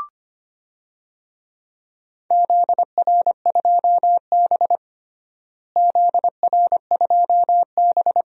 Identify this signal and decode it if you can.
Morse code, heard as a keyed tone, about 25 wpm, 700 Hz. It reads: ZR2B ZR2B
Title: DXCC-100 Call Signs - 4 Characters - Repeated 2x - 25wpm